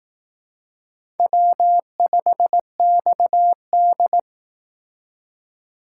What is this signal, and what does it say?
Morse code, heard as a keyed tone, about 19 wpm, 700 Hz. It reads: W5XD
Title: Speed Racing - USA Call Signs 15wpm